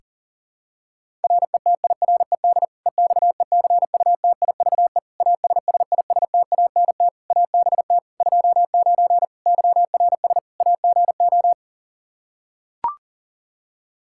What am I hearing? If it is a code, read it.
Morse code, heard as a keyed tone, about 40 wpm, 700 Hz. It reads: RETIRED EXECUTIVE ASSISTANT ABT 29 YRS AGO
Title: Practice QSO - Exchange 3 - More Info - 40wpm